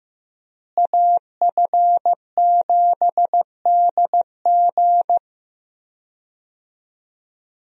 Morse code, heard as a keyed tone, about 16 wpm, 700 Hz. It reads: AF7DG
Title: USA Call Signs - 5 Characters - Repeated 2x - 15wpm